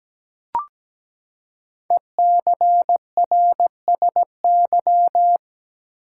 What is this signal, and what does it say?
Morse code, heard as a keyed tone, about 17 wpm, 700 Hz. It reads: ECRSY
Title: Code Groups - 5 Letters - 17wpm